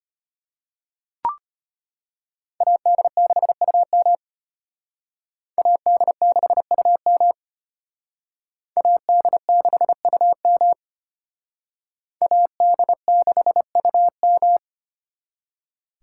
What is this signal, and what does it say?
Morse code, heard as a keyed tone, about 32 wpm, 700 Hz. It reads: AD6UM AD6UM AD6UM AD6UM